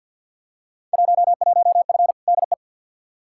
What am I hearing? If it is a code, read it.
Morse code, heard as a keyed tone, about 50 wpm, 700 Hz. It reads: 11F DE
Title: Field Day Exchanges Encore - 50wpm